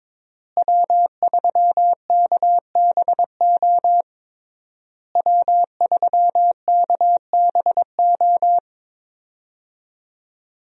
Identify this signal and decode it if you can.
Morse code, heard as a keyed tone, about 22 wpm, 700 Hz. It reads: W3KBO W3KBO
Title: USA Call Signs - 5 Characters - Repeated 3x - 22wpm